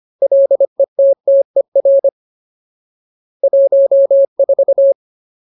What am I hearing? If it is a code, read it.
Morse code, heard as a keyed tone, about 25 wpm, 550 Hz. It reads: LETTER 14
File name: Ultimate Word List / 25WPM Ultimate